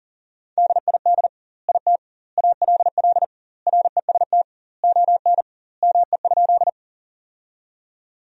Morse code, heard as a keyed tone, about 40 wpm, 700 Hz. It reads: DID IT ALL REST ON ME?
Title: 2x Word-Spacing - Sentences from Top 500 Words - 40wpm